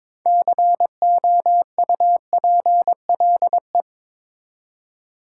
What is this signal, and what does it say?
Morse code, heard as a keyed tone, about 22 wpm, 700 Hz. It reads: COUPLE
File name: Build A Phrase - 2-Word Phrases - 22wpm